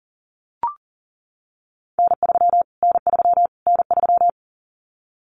Morse code, heard as a keyed tone, about 40 wpm, 700 Hz. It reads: N3 N3 N3
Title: Callsign Prefixes for Top 25 DXCC - Repeated 3x - 40wpm